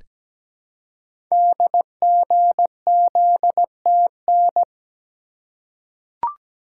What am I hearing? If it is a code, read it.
Morse code, heard as a keyed tone, about 17 wpm, 700 Hz. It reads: DGZTN
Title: Code Groups - 5 Letter:Numbers - 17wpm